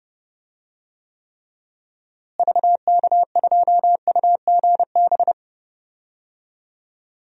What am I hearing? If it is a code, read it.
Morse code, heard as a keyed tone, about 31 wpm, 700 Hz. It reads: VK2UGB
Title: DXCC-100 Call Signs - 6 Characters - Repeated 3x - 30wpm